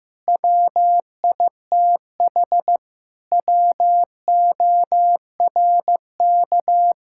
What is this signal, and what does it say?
Morse code, heard as a keyed tone, about 16 wpm, 700 Hz. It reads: WITH WORK
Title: Sentences from Top 1000 Words - Review Part 6 - 15wpm